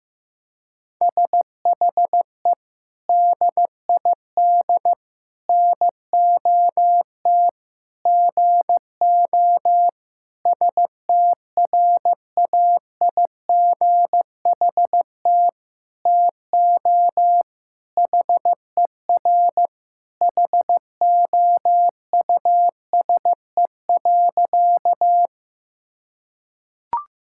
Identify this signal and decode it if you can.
Morse code, heard as a keyed tone, about 15 wpm, 700 Hz. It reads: SHE DID NOT GO STRAIGHT TO HER HOUSE.
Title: Sentences from Top 1000 Words - Review Part 6 - 15wpm